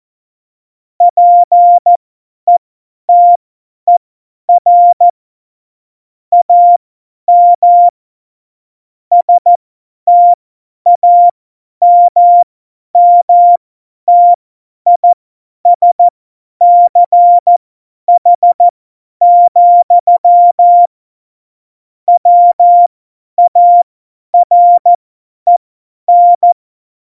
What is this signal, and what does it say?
Morse code, heard as a keyed tone, about 14 wpm, spaced as for 10 wpm, 700 Hz. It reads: PETER AM STAMMTISCH, WAREN
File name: Morsekurs / Witze / Witz 15. August 17, Tempo 50 F70